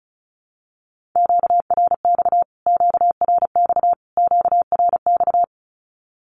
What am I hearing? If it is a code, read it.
Morse code, heard as a keyed tone, about 34 wpm, 700 Hz. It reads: QRX QRX QRX
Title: Q-Codes Repeated 3x 35wpm